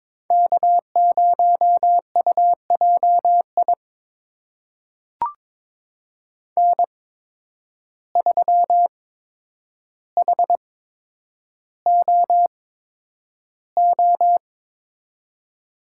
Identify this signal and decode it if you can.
Morse code, heard as a keyed tone, about 22 wpm, spaced as for 9 wpm, 700 Hz. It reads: K0UJI N3HOO
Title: USA Call Signs - 5 Characters - 8x Character Spacing - 22wpm